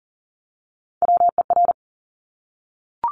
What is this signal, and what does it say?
Morse code, heard as a keyed tone, about 42 wpm, 700 Hz. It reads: WER
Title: Top Three Letter Combinations in Words 40wpm